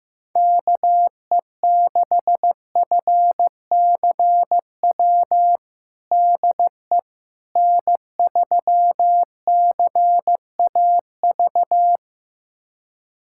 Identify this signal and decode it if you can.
Morse code, heard as a keyed tone, about 16 wpm, 700 Hz. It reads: KE6FCW DE N3CAV
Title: USA Callsign DE Callsign 15wpm